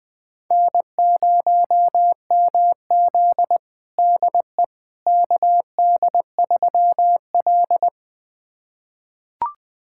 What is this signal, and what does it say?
Morse code, heard as a keyed tone, about 21 wpm, 700 Hz. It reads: N0MZ DE KD3L
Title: USA Callsign DE Callsign - 4 Characters - 20wpm